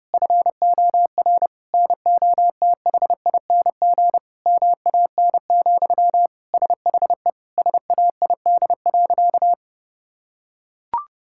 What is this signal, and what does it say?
Morse code, heard as a keyed tone, about 28 wpm, 700 Hz. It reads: FOR NOTHING MAN, SHE SAID.
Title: Sentences from Top 1000 Words - Review Part 6 - 30wpm